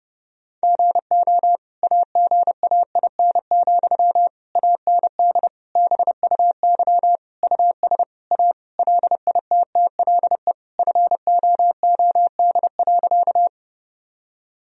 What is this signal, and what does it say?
Morse code, heard as a keyed tone, about 32 wpm, 700 Hz. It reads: GO AGAIN, AND BUY US A LITTLE FOOD.
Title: Sentences from Top 1000 Words - Review Part 4 - 30wpm